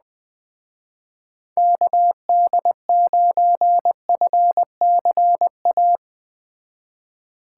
Morse code, heard as a keyed tone, about 21 wpm, 700 Hz. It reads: KD9FCA